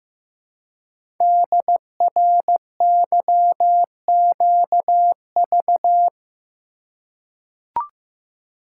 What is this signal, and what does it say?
Morse code, heard as a keyed tone, about 15 wpm, 700 Hz. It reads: DRYQV
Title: Code Groups - 5 Letters - 15wpm